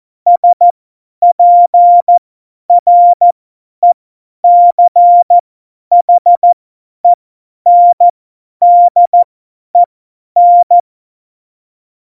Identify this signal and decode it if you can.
Morse code, heard as a keyed tone, about 14 wpm, spaced as for 10 wpm, 700 Hz. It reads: SPRECHENDEN
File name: Morsekurs / Witze / Witz 06. November 2018, 50 F70